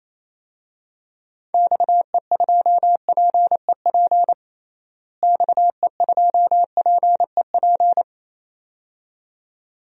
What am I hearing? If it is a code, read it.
Morse code, heard as a keyed tone, about 28 wpm, 700 Hz. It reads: XE2PEP XE2PEP
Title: DXCC-100 Call Signs - 6 Characters - Repeated 2x - 28wpm